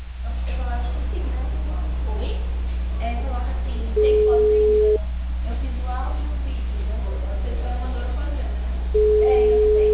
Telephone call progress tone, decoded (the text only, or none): ringback tone